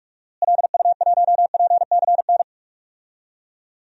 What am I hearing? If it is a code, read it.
Morse code, heard as a keyed tone, about 46 wpm, 700 Hz. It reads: LU1PCN
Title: DXCC-100 Call Signs - 6 Characters - Repeated 3x - 45wpm